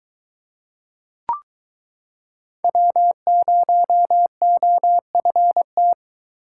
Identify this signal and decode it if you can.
Morse code, heard as a keyed tone, about 22 wpm, 700 Hz. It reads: W0OFT